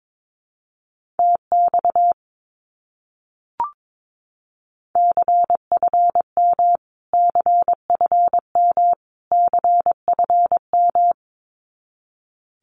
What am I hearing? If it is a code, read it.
Morse code, heard as a keyed tone, about 22 wpm, 700 Hz. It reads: TX CFM CFM CFM
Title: Abbreviations Repeated 3x - 22wpm